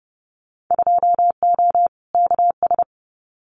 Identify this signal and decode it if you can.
Morse code, heard as a keyed tone, about 30 wpm, 700 Hz. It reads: 2O KS
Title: Winter Field Day Exchanges Encore 30wpm